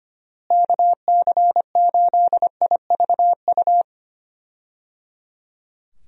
Morse code, heard as a keyed tone, about 25 wpm, 700 Hz. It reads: KC8IVU
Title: USA Call Signs - 6 Characters - Repeated 3x - 25wpm